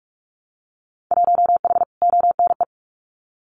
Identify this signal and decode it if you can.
Morse code, heard as a keyed tone, about 46 wpm, 700 Hz. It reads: 1H ONE
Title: Winter Field Day Exchanges Encore 45wpm